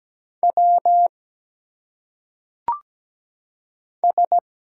Morse code, heard as a keyed tone, about 17 wpm, 700 Hz. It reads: W S
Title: Single Letters - Phonetic Pronunciation 17wpm